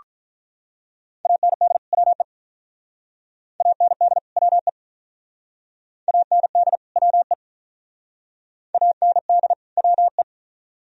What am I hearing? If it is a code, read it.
Morse code, heard as a keyed tone, about 44 wpm, 700 Hz. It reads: AND WE AND WE AND WE AND WE